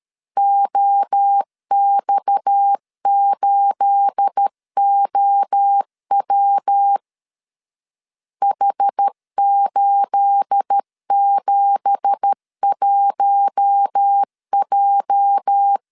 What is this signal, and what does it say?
Morse code, heard as a keyed tone, about 13 wpm, 800 Hz. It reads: OX8OW H871J
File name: CW / DET / 1 9 1500C S C 12wpm 3811